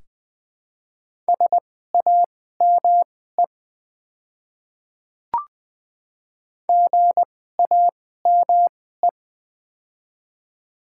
Morse code, heard as a keyed tone, about 20 wpm, spaced as for 14 wpm, 700 Hz. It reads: SAME GAME
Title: 2x Character Spacing - Top 500 Words - 20wpm